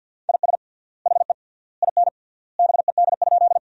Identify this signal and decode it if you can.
Morse code, heard as a keyed tone, about 50 wpm, 700 Hz. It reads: IS HE IN BED?
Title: 3x Word-Spacing - Sentences from Top 500 Words - 50wpm